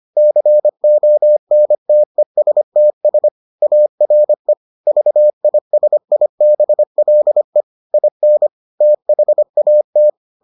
Morse code, heard as a keyed tone, about 25 wpm, 600 Hz. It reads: CONTESTS ARE VISIBLE IN THAT